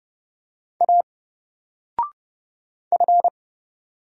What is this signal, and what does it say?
Morse code, heard as a keyed tone, about 30 wpm, 700 Hz. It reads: A F
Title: Single Character (Letter:Number:Punctuation:Prosign) 30wpm